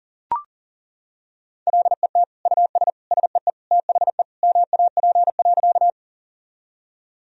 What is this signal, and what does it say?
Morse code, heard as a keyed tone, about 40 wpm, 700 Hz. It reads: LET US SEE THE MAP.
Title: Sentences from Top 1000 Words - Review Part 2 - 40wpm